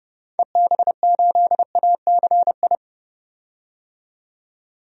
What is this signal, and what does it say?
Morse code, heard as a keyed tone, about 30 wpm, 700 Hz. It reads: EB8ACI
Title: DXCC-100 Call Signs - 6 Characters - 30wpm